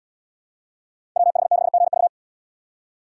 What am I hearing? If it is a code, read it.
Morse code, heard as a keyed tone, about 70 wpm, 700 Hz. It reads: WH6GV